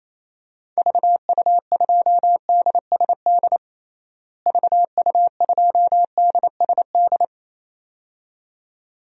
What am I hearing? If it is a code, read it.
Morse code, heard as a keyed tone, about 28 wpm, 700 Hz. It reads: VU2DSD VU2DSD
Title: DXCC-100 Call Signs - 6 Characters - Repeated 2x - 28wpm